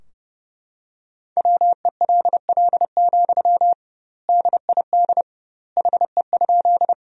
Morse code, heard as a keyed tone, about 30 wpm, 700 Hz. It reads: WELL, DID HE?
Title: 2x Word-Spacing - Sentences from Top 200 Words - 30wpm